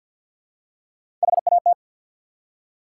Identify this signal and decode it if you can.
Morse code, heard as a keyed tone, about 51 wpm, 700 Hz. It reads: HAT
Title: Three Letter Words - 50wpm